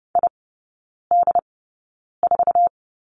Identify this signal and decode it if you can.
Morse code, heard as a keyed tone, about 30 wpm, spaced as for 9 wpm, 700 Hz. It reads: ID4
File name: course / 30:10wpm / 087 Lesson 15e - Number 5 - Three Characters 30wpm